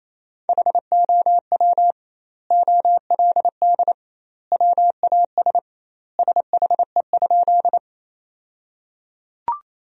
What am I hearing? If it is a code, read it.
Morse code, heard as a keyed tone, about 27 wpm, 700 Hz. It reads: HOW OLD WAS SHE?
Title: 2x Word-Spacing - Sentences from Top 200 Words - 28wpm